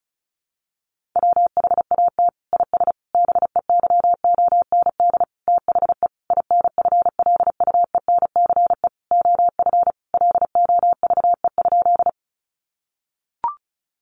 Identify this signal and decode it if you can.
Morse code, heard as a keyed tone, about 34 wpm, 700 Hz. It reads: WHAT IS BEYOND THE INFLUENCE OF LOVE?
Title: Sentences from Top 1100 Words 35wpm